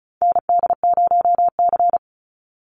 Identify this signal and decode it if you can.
Morse code, heard as a keyed tone, about 34 wpm, 700 Hz. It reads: ND0C
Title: NA SOTA Chaser Callsigns 35wpm